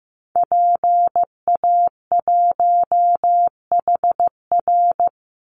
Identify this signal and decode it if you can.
Morse code, heard as a keyed tone, about 15 wpm, 700 Hz. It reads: PA1HR
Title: Call Signs Repeated 2x - 15wpm